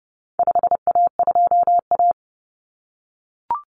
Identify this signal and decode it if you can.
Morse code, heard as a keyed tone, about 30 wpm, 700 Hz. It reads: 5A2A
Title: Common Cut Numbers - 4-Digit 30wpm